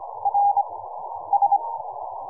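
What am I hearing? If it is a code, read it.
Morse code, heard as a keyed tone, about 24 wpm, 810 Hz. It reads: R S
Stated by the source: QRP Morse code signal.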